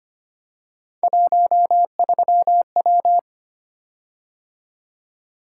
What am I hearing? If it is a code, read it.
Morse code, heard as a keyed tone, about 25 wpm, 700 Hz. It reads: 13W
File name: Three Letter-Number Combinations - 25wpm